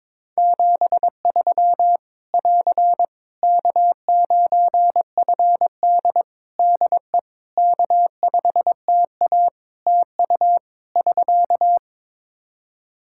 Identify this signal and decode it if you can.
Morse code, heard as a keyed tone, about 22 wpm, 700 Hz. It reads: 73 <AR> K9FD DE K5TA TU <SK>
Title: K1USN SST QSO Practice 22wpm